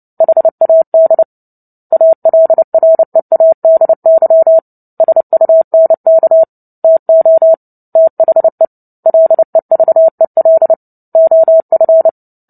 Morse code, heard as a keyed tone, about 30 wpm, 650 Hz. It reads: HAD ALREADY SUNK TO THE LEVEL OF
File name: Book Two / 03-Chapter 3 - 29 WPM